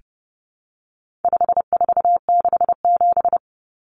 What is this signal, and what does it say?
Morse code, heard as a keyed tone, about 30 wpm, 700 Hz. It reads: 5467
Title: Common Cut Numbers - 4-Digit 30wpm